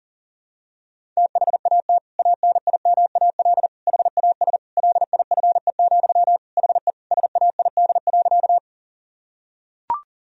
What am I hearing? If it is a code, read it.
Morse code, heard as a keyed tone, about 37 wpm, 700 Hz. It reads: THAT ANIMAL HAS LIFE, HE SAID.
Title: Sentences from Top 1000 Words - Review Part 1 - 40wpm